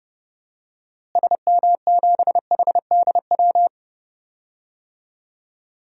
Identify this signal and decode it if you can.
Morse code, heard as a keyed tone, about 30 wpm, 700 Hz. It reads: SM7HDW
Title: DXCC-100 Call Signs - 6 Characters - Repeated 2x - 30wpm